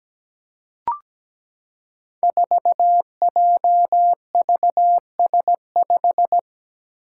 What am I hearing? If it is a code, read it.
Morse code, heard as a keyed tone, about 17 wpm, 700 Hz. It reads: 4JVS5
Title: Five Challenging Letter-Number Combinations 17wpm